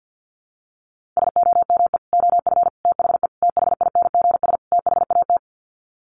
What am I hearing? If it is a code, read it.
Morse code, heard as a keyed tone, about 50 wpm, 700 Hz. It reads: SOME OF THE THINGS THAT
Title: Sets of 5 Words 50wpm